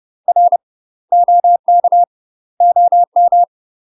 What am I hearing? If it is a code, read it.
Morse code, heard as a keyed tone, about 30 wpm, 700 Hz. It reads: R OK OM
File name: UK MASTER QSOs TRAINING / 30WPM / UK MASTER QSOs 0030